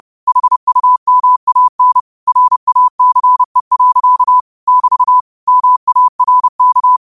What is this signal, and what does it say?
Morse code, heard as a keyed tone, about 30 wpm, 1000 Hz. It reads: HUMAN RACE. <BT> MARK
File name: qotdfiles / 30wpm